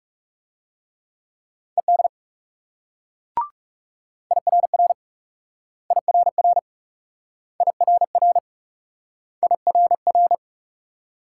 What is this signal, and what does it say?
Morse code, heard as a keyed tone, about 36 wpm, 700 Hz. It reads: ED IRR IRR IRR IRR
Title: Speed-Racing Multi-letter Phonemes - 30wpm